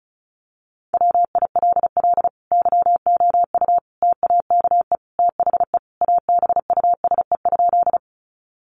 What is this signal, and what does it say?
Morse code, heard as a keyed tone, about 34 wpm, 700 Hz. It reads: WILL YOU TAKE THE ABUSE?
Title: Sentences from Top 1300 Words 35wpm